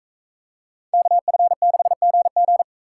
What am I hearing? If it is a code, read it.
Morse code, heard as a keyed tone, about 42 wpm, 700 Hz. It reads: KF6GG